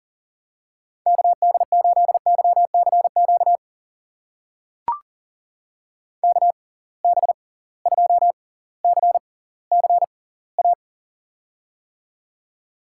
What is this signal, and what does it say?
Morse code, heard as a keyed tone, about 38 wpm, spaced as for 20 wpm, 700 Hz. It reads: KD8YCQ KB2CCA
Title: USA Call Signs - 6 Characters - 6x Character Spacing - 40wpm